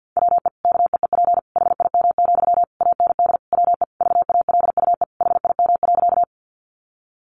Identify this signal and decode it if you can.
Morse code, heard as a keyed tone, about 50 wpm, 700 Hz. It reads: WE KEEP HIM, AND WE VALUE HIM.
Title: Sentences from Top 600 Words 50wpm